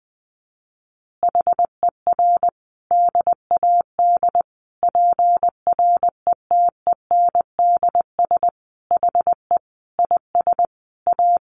Morse code, heard as a keyed tone, about 20 wpm, 700 Hz. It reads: HER DAD PRETENDS HE IS A
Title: Additional Easy Phrases 20wpm